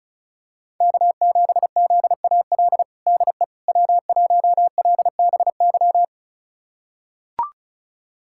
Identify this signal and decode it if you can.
Morse code, heard as a keyed tone, about 36 wpm, 700 Hz. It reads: K7ZAL DE W1LBY
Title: USA Callsign DE Callsign - 5 Characters - 35wpm